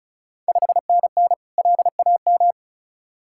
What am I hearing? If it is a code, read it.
Morse code, heard as a keyed tone, about 36 wpm, 690 Hz. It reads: 5NN LAM
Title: Ontario QSO Party Exchange 35wpm